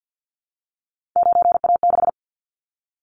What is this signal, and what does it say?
Morse code, heard as a keyed tone, about 50 wpm, 700 Hz. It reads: 9A6
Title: Callsign Prefixes for Top 25 DXCC - 50wpm